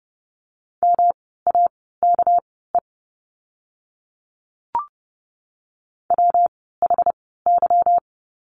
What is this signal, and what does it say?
Morse code, heard as a keyed tone, about 30 wpm, spaced as for 17 wpm, 700 Hz. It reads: MAKE WHY
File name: course / 30:10wpm / 237 Lesson 38f - Letter q - Words 30wpm